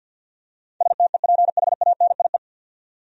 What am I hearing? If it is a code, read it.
Morse code, heard as a keyed tone, about 50 wpm, 700 Hz. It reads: STEPHANIE
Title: USA Common Names - 50wpm